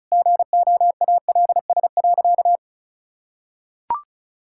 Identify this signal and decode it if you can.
Morse code, heard as a keyed tone, about 36 wpm, 700 Hz. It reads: GOALS.
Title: Sentences from Top 1000 Words - Review Part 6 - 35wpm